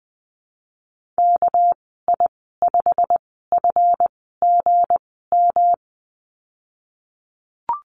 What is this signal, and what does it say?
Morse code, heard as a keyed tone, about 19 wpm, spaced as for 14 wpm, 700 Hz. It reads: KI5FGM
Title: 207 Lesson 33g - Letter j - Call Signs 20wpm